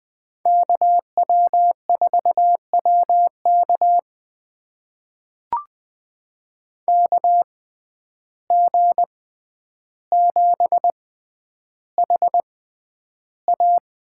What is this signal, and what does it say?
Morse code, heard as a keyed tone, about 20 wpm, spaced as for 10 wpm, 700 Hz. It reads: KW4WK KG7HA
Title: USA Call Signs - 5 Characters - 6x Character Spacing - 20wpm